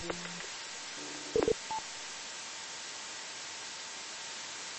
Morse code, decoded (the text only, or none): none